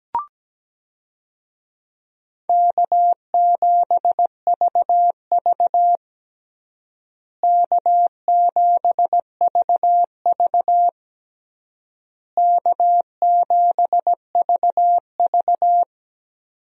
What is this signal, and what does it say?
Morse code, heard as a keyed tone, about 17 wpm, 700 Hz. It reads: K7VV K7VV K7VV
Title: USA Call Signs - 4 Characters - Repeated 3x - 17wpm